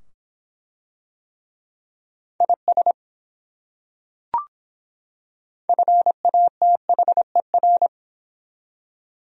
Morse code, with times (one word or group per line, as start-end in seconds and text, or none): 2.4-2.9 IS
5.7-7.9 FATHER